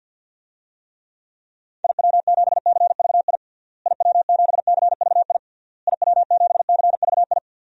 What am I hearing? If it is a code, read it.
Morse code, heard as a keyed tone, about 52 wpm, 700 Hz. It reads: IW7CVI IW7CVI IW7CVI
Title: DXCC-100 Call Signs - 6 Characters - Repeated 3x - 50wpm